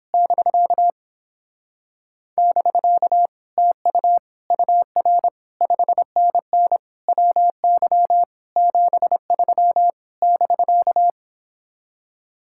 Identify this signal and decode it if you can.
Morse code, heard as a keyed tone, about 29 wpm, 700 Hz. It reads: <BK> <BK> TU UR 5NN WY 73 <BK>